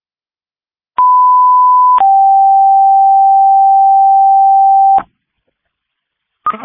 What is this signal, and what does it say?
Two-tone page, heard as a one-second tone, then a three-second tone, about 1000 then 780 Hz. It is heard from 1.0 to 5.0 s.